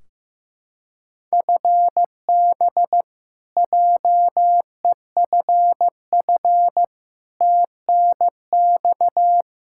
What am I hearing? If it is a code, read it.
Morse code, heard as a keyed tone, about 15 wpm, 700 Hz. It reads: FB JEFF TNX
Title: Practice QSO - Exchange 3 - More Info - 15wpm